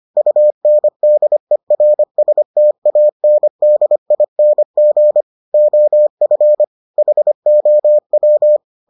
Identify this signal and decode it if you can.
Morse code, heard as a keyed tone, about 25 wpm, 600 Hz. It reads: UNDERSTANDING OF HOW